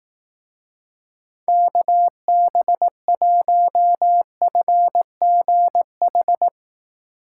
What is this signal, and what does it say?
Morse code, heard as a keyed tone, about 18 wpm, 700 Hz. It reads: KB1FGH
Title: Speed Racing - USA 2x3 Call Signs 15wpm